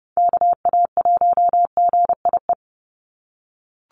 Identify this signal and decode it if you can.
Morse code, heard as a keyed tone, about 28 wpm, 700 Hz. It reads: KA1GIE
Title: USA Call Signs 30wpm